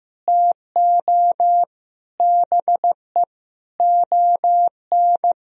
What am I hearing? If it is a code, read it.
Morse code, heard as a keyed tone, about 15 wpm, 700 Hz. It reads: TO BE ON
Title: Sentences from Top 1000 Words - Review Part 2 - 15wpm